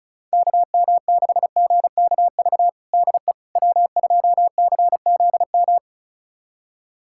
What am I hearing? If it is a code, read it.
Morse code, heard as a keyed tone, about 36 wpm, 700 Hz. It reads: KM6GKV DE W2CZM
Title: USA Callsign DE Callsign 35wpm